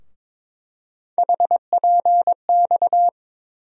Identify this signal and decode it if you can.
Morse code, heard as a keyed tone, about 21 wpm, 700 Hz. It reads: HPX
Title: USA Call Sign Postfixes - 3 Letters 22wpm